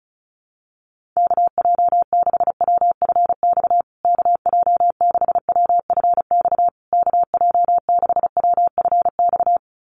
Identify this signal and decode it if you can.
Morse code, heard as a keyed tone, about 34 wpm, 700 Hz. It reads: KJ6WFX KJ6WFX KJ6WFX
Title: USA Call Signs Repeated 3x 35wpm